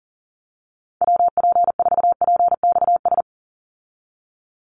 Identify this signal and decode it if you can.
Morse code, heard as a keyed tone, about 40 wpm, 700 Hz. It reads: WP4PXS
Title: USA Call Signs 40wpm